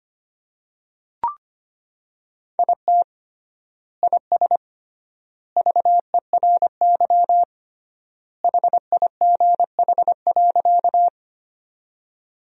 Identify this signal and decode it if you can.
Morse code, heard as a keyed tone, about 25 wpm, 700 Hz. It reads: IT IS VERY HIGH.
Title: 3x Word-Spacing - Sentences from Top 200 Words - 25wpm